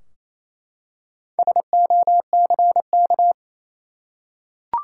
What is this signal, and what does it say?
Morse code, heard as a keyed tone, about 28 wpm, 700 Hz. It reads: SOCK
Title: Four Letter Words - 28wpm